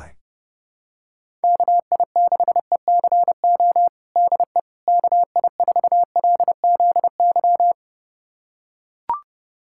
Morse code, heard as a keyed tone, about 30 wpm, 700 Hz. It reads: KI6ECO DE KI4LZY
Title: USA Callsign DE Callsign - 6 Characters - 30wpm